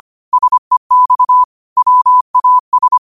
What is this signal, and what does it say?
Morse code, heard as a keyed tone, about 25 wpm, 1000 Hz. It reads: SEX WAS
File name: qotdfiles / 25wpm